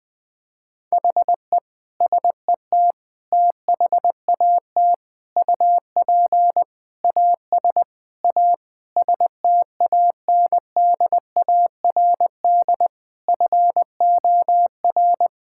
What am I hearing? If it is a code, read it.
Morse code, heard as a keyed tone, about 21 wpm, 700 Hz. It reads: HE SET THAT UP AS A STANDARD FOR
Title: Sentences from Top 1000 Words - Review Part 6 - 20wpm